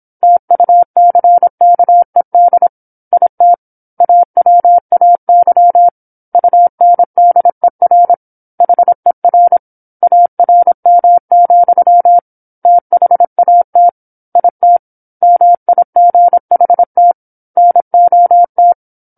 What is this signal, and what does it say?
Morse code, heard as a keyed tone, about 26 wpm, 700 Hz. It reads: TUCKED IT AWAY UNDER HER ARM, THAT IT MIGHT NOT
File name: 08-Chapter-08 26 WPM